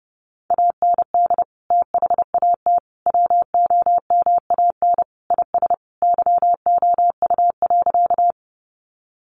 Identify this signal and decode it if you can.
Morse code, heard as a keyed tone, about 30 wpm, 700 Hz. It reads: AND THAT WOMAN IS YOU.
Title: Sentences from Top 1100 Words 30wpm